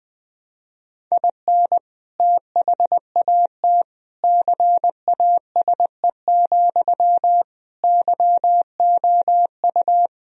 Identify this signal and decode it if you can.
Morse code, heard as a keyed tone, about 20 wpm, 700 Hz. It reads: IN THAT CASE, YOU
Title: Sentences from Top 1000 Words - Review Part 1 - 20wpm